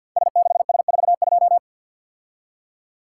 Morse code, heard as a keyed tone, about 51 wpm, 690 Hz. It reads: S6S42
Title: Code Groups - 5 Letter:Numbers - 50wpm